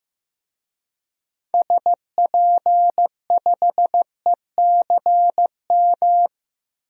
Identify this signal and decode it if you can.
Morse code, heard as a keyed tone, about 15 wpm, 700 Hz. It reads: SP5ECM
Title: DXCC-100 Call Signs - 6 Characters - 15wpm